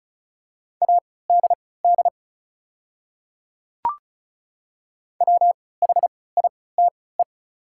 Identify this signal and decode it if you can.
Morse code, heard as a keyed tone, about 36 wpm, spaced as for 20 wpm, 690 Hz. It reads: ADD WHITE
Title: 3x Character Spacing - Top 300 Words - 35wpm